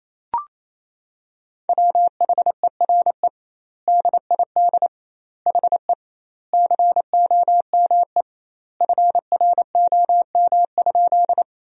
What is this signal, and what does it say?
Morse code, heard as a keyed tone, about 27 wpm, 700 Hz. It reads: WHERE DID HE COME FROM?
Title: 2x Word-Spacing - Sentences from Top 200 Words - 28wpm